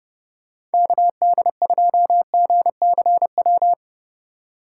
Morse code, heard as a keyed tone, about 30 wpm, 700 Hz. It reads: KD2GCW
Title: USA Call Signs - 6 Characters - 3x Character Spacing - 30wpm